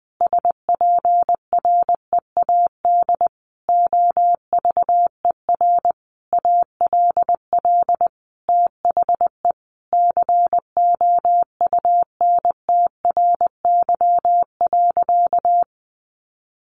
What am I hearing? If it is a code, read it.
Morse code, heard as a keyed tone, about 20 wpm, 700 Hz. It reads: SPREAD OVER ALL THE COUNTRY.
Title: Sentences from Top 1200 Words 20wpm